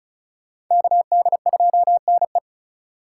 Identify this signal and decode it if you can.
Morse code, heard as a keyed tone, about 45 wpm, 700 Hz. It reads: KD2NE